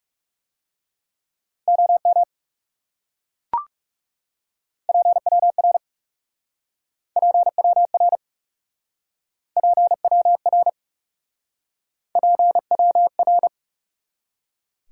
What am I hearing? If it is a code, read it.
Morse code, heard as a keyed tone, about 40 wpm, 700 Hz. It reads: OM PWR PWR PWR PWR